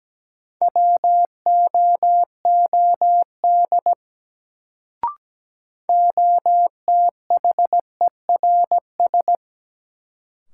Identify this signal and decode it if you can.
Morse code, heard as a keyed tone, about 17 wpm, 700 Hz. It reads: WOOD OTHERS
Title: Sentences from Top 1000 Words - Review Part 2 - New Word Drill - 17wpm